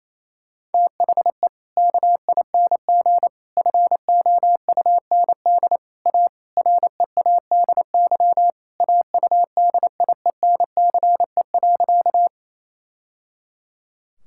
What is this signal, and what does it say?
Morse code, heard as a keyed tone, about 28 wpm, 700 Hz. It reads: THE KING FOUND A READY AUDIENCE.
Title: Sentences from Top 1000 Words - Review Part 5 - 28wpm